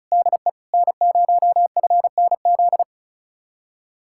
Morse code, heard as a keyed tone, about 44 wpm, 700 Hz. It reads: DE N0FNZ
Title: Speed-Racing Common QSO Phrases - 35wpm